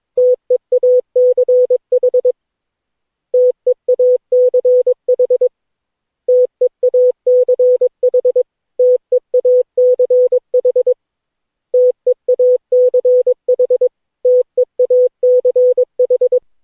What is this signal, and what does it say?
Morse code, heard as a keyed tone, about 22 wpm, 500 Hz. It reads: TEACH TEACH TEACH TEACH TEACH TEACH
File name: cw / Session9 cw training